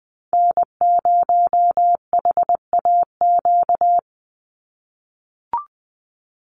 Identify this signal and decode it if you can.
Morse code, heard as a keyed tone, about 19 wpm, 700 Hz. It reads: N0HAQ
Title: USA Call Signs 20wpm